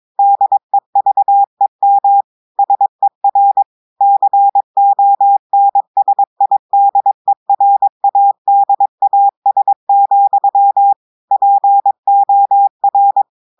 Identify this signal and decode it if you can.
Morse code, heard as a keyed tone, about 22 wpm, 800 Hz. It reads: DEVEM SER CONSIDERADAS, POR
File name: audios / portugais / portugais texte en clair 2 vitesse 22 mots minutes